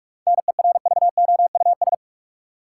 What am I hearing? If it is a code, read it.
Morse code, heard as a keyed tone, about 45 wpm, 700 Hz. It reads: NERVOUS
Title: Seven Letter Words - 45wpm